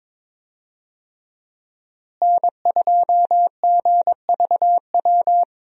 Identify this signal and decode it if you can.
Morse code, heard as a keyed tone, about 21 wpm, 700 Hz. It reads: N2GVW
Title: USA Call Signs - 5 Characters - Repeated 3x - 22wpm